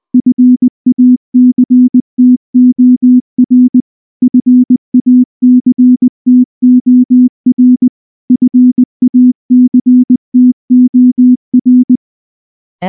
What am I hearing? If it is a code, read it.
Morse code, heard as a keyed tone, about 19 wpm, 250 Hz. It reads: FACTOR FACTOR FACTOR
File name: morse / morse-training-20WPM-10